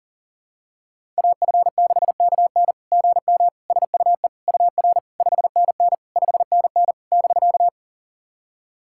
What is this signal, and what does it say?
Morse code, heard as a keyed tone, about 40 wpm, 690 Hz. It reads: AF6KN GM SUE UR 5NN 5NN <BK>
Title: SOTA QSO Practice 40wpm